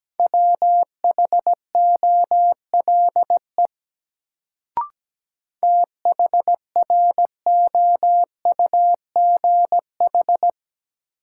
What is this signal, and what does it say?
Morse code, heard as a keyed tone, about 17 wpm, 700 Hz. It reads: WHOLE THROUGH
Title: Sets of 2 Words - New Word Drill - 17wpm